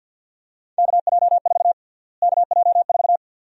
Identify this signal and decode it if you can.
Morse code, heard as a keyed tone, about 52 wpm, 700 Hz. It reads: KJ4 KJ4
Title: USA Call Sign Prefixes - Repeated 2x - 50wpm